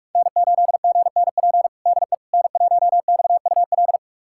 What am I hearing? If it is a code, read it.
Morse code, heard as a keyed tone, about 46 wpm, 700 Hz. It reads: N8GNP DE N1XUL
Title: USA Callsign DE Callsign - 5 Characters - 45wpm